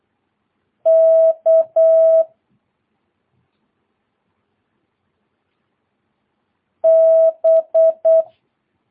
Morse code, heard as a keyed tone, about 8 wpm, 650 Hz. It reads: K B
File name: jh8cbh / 2022cw